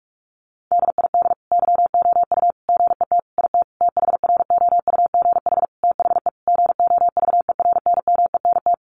Morse code, heard as a keyed tone, about 44 wpm, 700 Hz. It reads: DID YOU GET IT THROUGH THE GOVERNMENT
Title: Sentences from Top 1300 Words 45wpm